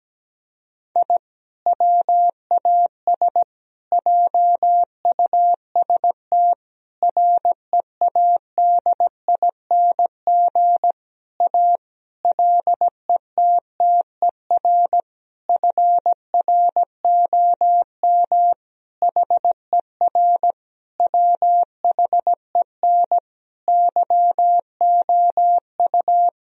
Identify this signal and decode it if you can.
Morse code, heard as a keyed tone, about 17 wpm, 700 Hz. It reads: I WAS JUST READING A LETTER FROM HER WHEN YOU
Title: Sentences from Top 1000 Words - Review Part 3 - 17wpm